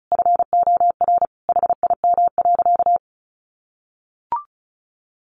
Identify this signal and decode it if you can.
Morse code, heard as a keyed tone, about 34 wpm, 700 Hz. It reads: FOR HIM.
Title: Sentences from Top 1100 Words 35wpm